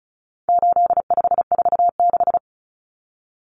Morse code, heard as a keyed tone, about 34 wpm, 700 Hz. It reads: 8546
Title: Common Cut Numbers - 4-Digit 35wpm